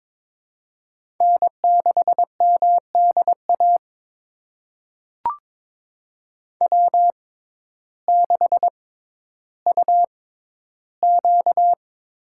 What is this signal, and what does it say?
Morse code, heard as a keyed tone, about 23 wpm, spaced as for 12 wpm, 700 Hz. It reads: N6MDA W6UQ
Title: USA Call Signs - 5 Characters - 6x Character Spacing - 22wpm